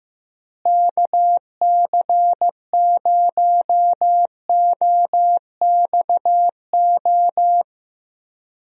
Morse code, heard as a keyed tone, about 15 wpm, 700 Hz. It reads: KC0OXO
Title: USA Call Signs - 6 Characters - Repeated 2x - 15wpm